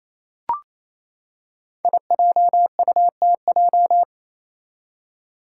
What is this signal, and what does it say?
Morse code, heard as a keyed tone, about 29 wpm, 700 Hz. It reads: IJUTJ
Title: Five Challenging Letter-Number Combinations 28wpm